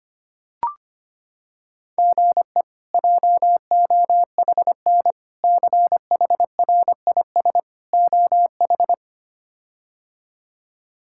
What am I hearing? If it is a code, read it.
Morse code, heard as a keyed tone, about 25 wpm, 700 Hz. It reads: GE JOHN CHRIS OH
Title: K1USN SST QSO Practice - 25wpm